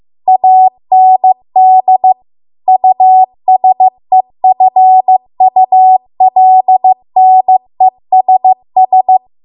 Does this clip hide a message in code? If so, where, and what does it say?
Yes: Morse code from 0.3 to 9.2 s, heard as a keyed tone, about 15 wpm, 750 Hz. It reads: AND USEFULNESS